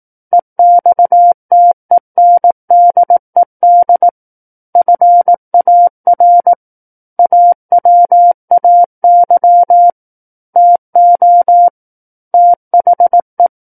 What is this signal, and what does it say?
Morse code, heard as a keyed tone, about 18 wpm, 700 Hz. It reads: EXTENDED FAR AWAY TO THE
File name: Book One / 09-Chapter 9 - 18 WPM